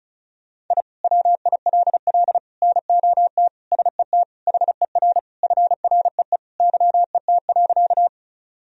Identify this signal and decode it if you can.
Morse code, heard as a keyed tone, about 36 wpm, 700 Hz. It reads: I WILL NOT SET HER FREE YET.
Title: Sentences from Top 1000 Words - Review Part 1 - 35wpm